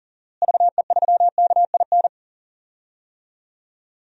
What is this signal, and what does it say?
Morse code, heard as a keyed tone, about 40 wpm, 700 Hz. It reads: VE3KIN
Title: DXCC-100 Call Signs - 6 Characters - Repeated 3x - 40wpm